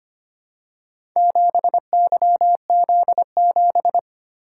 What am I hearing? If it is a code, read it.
Morse code, heard as a keyed tone, about 25 wpm, 700 Hz. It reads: 7YZ7
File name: practice / 25wpm / Four Letter-Number Combinations - 25wpm